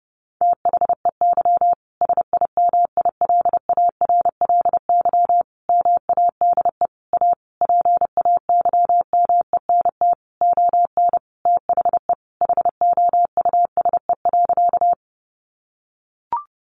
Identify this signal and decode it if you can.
Morse code, heard as a keyed tone, about 30 wpm, 700 Hz. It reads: THEY SIMILARLY MADE A PAYMENT ON THE HOUSE.
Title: Sentences from Top 1400 Words 30wpm